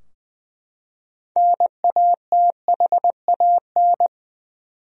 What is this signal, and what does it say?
Morse code, heard as a keyed tone, about 21 wpm, 700 Hz. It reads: NATHAN
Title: USA Common Names - 20wpm